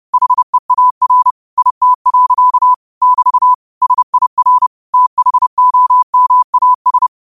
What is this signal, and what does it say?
Morse code, heard as a keyed tone, about 30 wpm, 1000 Hz. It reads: HEAR IT. <BT> SIR THOMAS